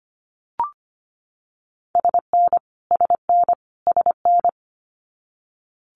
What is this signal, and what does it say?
Morse code, heard as a keyed tone, about 25 wpm, 700 Hz. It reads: SN SN SN
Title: Multi-letter Phonemes Repeated 3x 25wpm